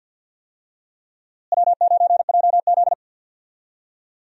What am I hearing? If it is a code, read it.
Morse code, heard as a keyed tone, about 52 wpm, 700 Hz. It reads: W9JZ
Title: USA Call Signs - 4 Characters - Repeated 2x - 50wpm